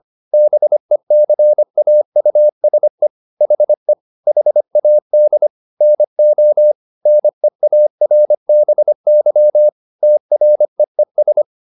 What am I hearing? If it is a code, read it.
Morse code, heard as a keyed tone, about 25 wpm, 600 Hz. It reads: BECAUSE HE HAD NO NEARBY TREES
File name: SS202 25